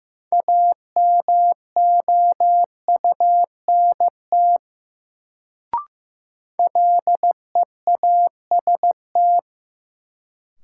Morse code, heard as a keyed tone, about 15 wpm, 700 Hz. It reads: AMOUNT LEAST
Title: Sentences from Top 700 Words - New Word Drill - 15wpm